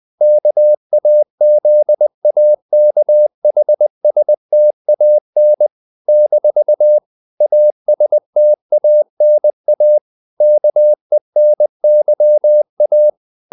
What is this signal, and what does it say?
Morse code, heard as a keyed tone, about 20 wpm, 600 Hz. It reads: KAZAKHSTAN - ASTANA KENYA
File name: CW / Practice / NationalCapitals20